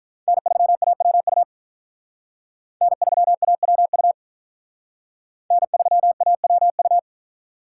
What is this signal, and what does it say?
Morse code, heard as a keyed tone, about 46 wpm, 700 Hz. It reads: N3AWU N3AWU N3AWU